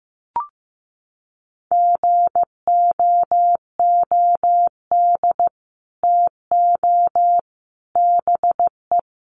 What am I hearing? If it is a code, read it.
Morse code, heard as a keyed tone, about 15 wpm, 700 Hz. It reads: GOOD TO BE
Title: Sets of 5 Words Encore2 15wpm